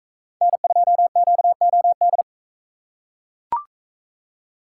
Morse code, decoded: N2QOD

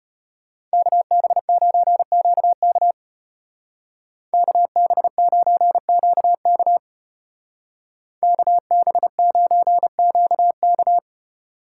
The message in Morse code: KB9QK KB9QK KB9QK

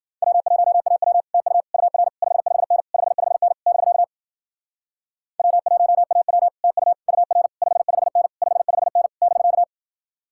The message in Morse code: W1AW TU UR 55N 55N <BK> W1AW TU UR 55N 55N <BK>

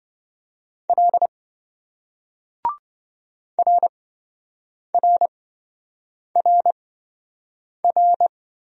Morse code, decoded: L RRRR

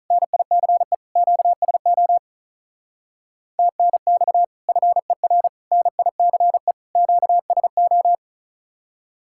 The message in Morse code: NICE QSO TNX FER NICE QSO

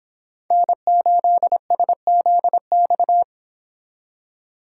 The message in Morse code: N8SZX

